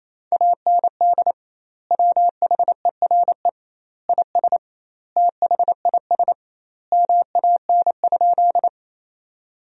Morse code, heard as a keyed tone, about 28 wpm, 700 Hz. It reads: AND WHERE IS THIS MAN?
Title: 2x Word-Spacing - Sentences from Top 200 Words - 28wpm